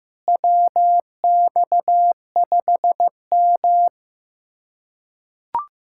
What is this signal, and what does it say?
Morse code, heard as a keyed tone, about 15 wpm, 700 Hz. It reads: WX5M
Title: USA Callsign DE Callsign - 4 Characters - 15wpm